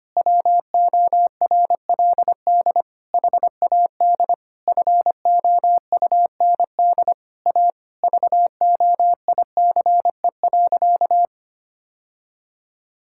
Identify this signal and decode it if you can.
Morse code, heard as a keyed tone, about 25 wpm, 700 Hz. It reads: WORLD HAD FOUND A VOICE.
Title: Sentences from Top 1600 Words 25wpm